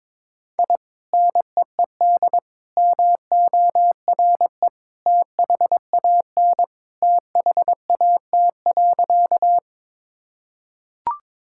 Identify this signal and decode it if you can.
Morse code, heard as a keyed tone, about 22 wpm, 700 Hz. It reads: I NEED MORE THAN THAT.
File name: practice / 22wpm / Sentences with 2-Word Phrases - 22wpm